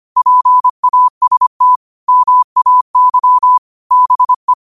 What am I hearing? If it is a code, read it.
Morse code, heard as a keyed tone, about 25 wpm, 1000 Hz. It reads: PAST MAY BE
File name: qotdfiles / 25wpm